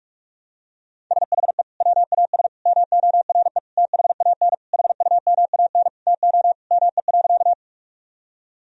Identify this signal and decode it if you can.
Morse code, heard as a keyed tone, about 47 wpm, 700 Hz. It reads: SHE WAS MORE THAN HUMAN TO ME.
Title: Sentences from Top 1000 Words - Review Part 3 - 45wpm